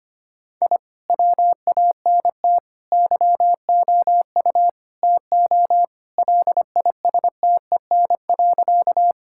Morse code, heard as a keyed tone, about 25 wpm, 700 Hz. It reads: I WANT YOU TO LISTEN.
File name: practice / 25wpm / Sentences with 4-Word Phrases - 25wpm